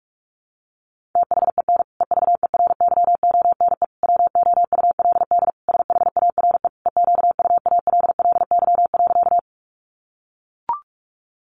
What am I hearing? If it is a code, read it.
Morse code, heard as a keyed tone, about 44 wpm, 700 Hz. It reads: THEN EVERYONE WOULD SHARE EQUALLY.
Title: Sentences from Top 1400 Words 45wpm